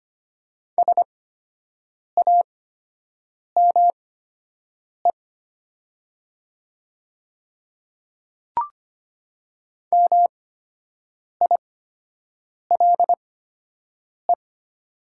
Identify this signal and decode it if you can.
Morse code, heard as a keyed tone, about 25 wpm, spaced as for 7 wpm, 700 Hz. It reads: SAME MILE
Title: 8x Character Spacing - Top 300 Words - 25wpm